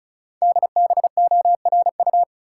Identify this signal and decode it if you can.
Morse code, heard as a keyed tone, about 36 wpm, 700 Hz. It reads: DBORU
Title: Code Groups - 5 Letters - 35wpm